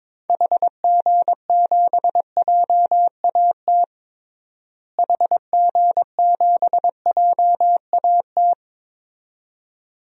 Morse code, heard as a keyed tone, about 21 wpm, 700 Hz. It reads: HG7JAT HG7JAT
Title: DXCC-100 Call Signs - 6 Characters - Repeated 2x - 22wpm